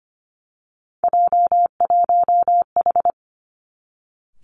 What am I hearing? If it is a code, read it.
Morse code, heard as a keyed tone, about 25 wpm, 700 Hz. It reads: J1H
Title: Three Challenging Letter-Number Combinations 25wpm